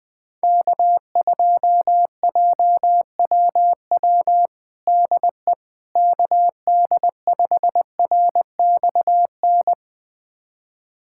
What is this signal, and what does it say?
Morse code, heard as a keyed tone, about 21 wpm, 700 Hz. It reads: K2JWW DE KD5RXN
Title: USA Callsign DE Callsign 20wpm